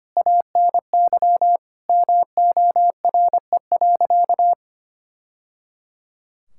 Morse code, heard as a keyed tone, about 25 wpm, 700 Hz. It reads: ANY MORE.
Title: Sentences from Top 1000 Words - Review Part 5 - 25wpm